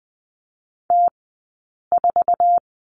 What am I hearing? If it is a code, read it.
Morse code, heard as a keyed tone, about 20 wpm, spaced as for 8 wpm, 700 Hz. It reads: T4
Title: 183 Lesson 30d - Letter b - Two Characters 20wpm